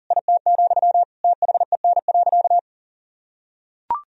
Morse code, heard as a keyed tone, about 40 wpm, 700 Hz. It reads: IT, THEN.
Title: Sentences from Top 200 Words and Abbreviations 40wpm